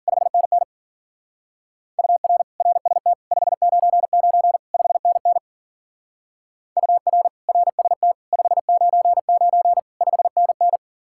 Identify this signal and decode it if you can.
Morse code, heard as a keyed tone, about 49 wpm, 700 Hz. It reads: 5NN UR RST 599 5NN UR RST 599 5NN
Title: Speed-Racing Common QSO Phrases Encore - 40wpm